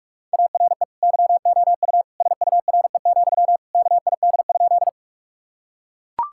Morse code, heard as a keyed tone, about 42 wpm, 700 Hz. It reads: ARE YOU SURE, KID?
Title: Sentences from Top 1600 Words 45wpm